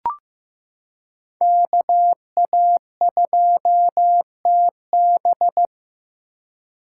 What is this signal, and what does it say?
Morse code, heard as a keyed tone, about 15 wpm, 700 Hz. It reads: KA2TB
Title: Code Groups - 5 Letter:Numbers - 15wpm